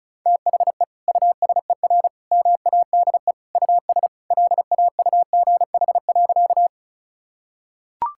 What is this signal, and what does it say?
Morse code, heard as a keyed tone, about 36 wpm, 700 Hz. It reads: THE USER MADE US LAUGH.
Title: Sentences from Top 1000 Words - Review Part 5 - 35wpm